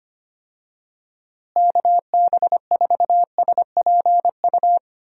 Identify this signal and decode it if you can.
Morse code, heard as a keyed tone, about 25 wpm, 700 Hz. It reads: KB4SPU
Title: USA Call Signs - 6 Characters - Repeated 3x - 25wpm